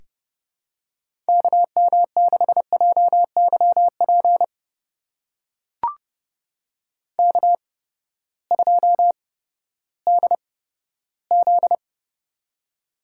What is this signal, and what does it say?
Morse code, heard as a keyed tone, about 30 wpm, spaced as for 15 wpm, 700 Hz. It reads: KM6JYP K2DZ
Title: USA Call Signs - 8x Character Spacing - 30wpm